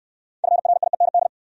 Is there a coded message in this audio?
Yes: Morse code from 0.4 to 1.3 s, heard as a keyed tone, about 62 wpm, 700 Hz. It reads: FRIEND